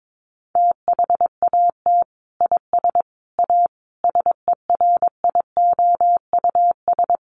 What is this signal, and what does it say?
Morse code, heard as a keyed tone, about 22 wpm, 700 Hz. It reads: THAT IS A SERIOUS
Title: Sentences from Top 1100 Words 22wpm